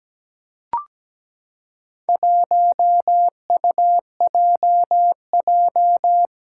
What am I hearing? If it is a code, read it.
Morse code, heard as a keyed tone, about 17 wpm, 700 Hz. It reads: 1UJJ
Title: Four Challenging Letter-Number Combinations 17wpm